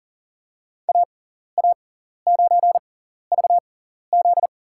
Morse code, heard as a keyed tone, about 40 wpm, spaced as for 14 wpm, 700 Hz. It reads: AA9VZ
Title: USA Call Signs - 5 Characters - 6x Character Spacing - 40wpm